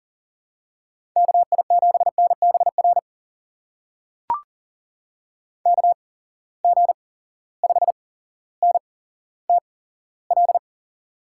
Morse code, heard as a keyed tone, about 40 wpm, spaced as for 17 wpm, 700 Hz. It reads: KI7NBR KG5NTL
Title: USA Call Signs - 8x Character Spacing - 40wpm